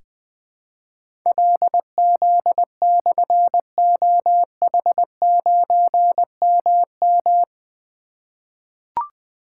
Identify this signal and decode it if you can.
Morse code, heard as a keyed tone, about 20 wpm, 700 Hz. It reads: LZ/OH9MM
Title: Difficult Call Signs 20wpm